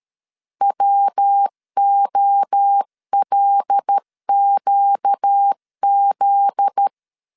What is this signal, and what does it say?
Morse code, heard as a keyed tone, about 13 wpm, 800 Hz. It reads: WOLQZ
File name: LEZIONI / 7 GF9Z / 7 GF9Z 12wpm 2147